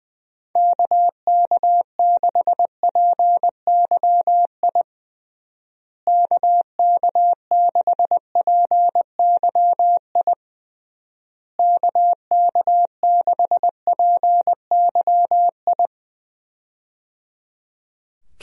Morse code, heard as a keyed tone, about 20 wpm, 700 Hz. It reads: KK6PYI KK6PYI KK6PYI
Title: USA Call Signs - 6 Characters - Repeated 3x - 20wpm